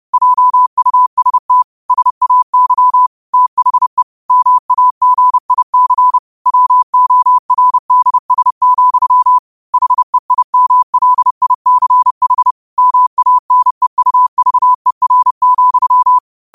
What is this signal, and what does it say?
Morse code, heard as a keyed tone, about 30 wpm, 1000 Hz. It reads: JUST SAY THE MAGIC WORDS, HEIMLICH MANEUVER,
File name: qotdfiles / 30wpm